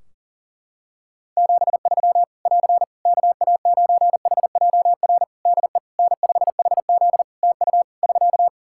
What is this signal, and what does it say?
Morse code, heard as a keyed tone, about 40 wpm, 700 Hz. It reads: 73 <AR> KA9HJR DE N5HZ TU <SK>
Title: Practice QSO - Exchange 4 - Ending - 40wpm